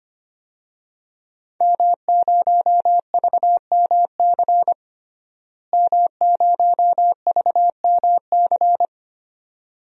Morse code, heard as a keyed tone, about 25 wpm, 700 Hz. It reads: M0VMC M0VMC
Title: DXCC-100 Call Signs - 5 Characters - Repeated 2x - 25wpm